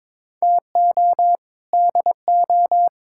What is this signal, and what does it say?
Morse code, heard as a keyed tone, about 22 wpm, 700 Hz. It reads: TO DO
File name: practice / 22wpm / Sets of 6 Words - 22wpm